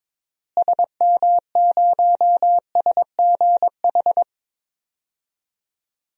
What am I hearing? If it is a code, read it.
Morse code, heard as a keyed tone, about 21 wpm, 700 Hz. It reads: SM0SGH
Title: DXCC-100 Call Signs - 6 Characters - Repeated 2x - 22wpm